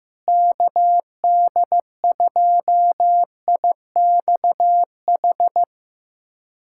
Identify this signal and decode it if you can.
Morse code, heard as a keyed tone, about 15 wpm, 700 Hz. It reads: KD2IXH
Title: USA Callsign DE Callsign - 6 Characters - 15wpm